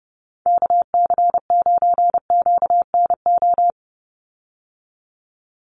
Morse code, heard as a keyed tone, about 30 wpm, 700 Hz. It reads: KC9QNO
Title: USA Call Signs 30wpm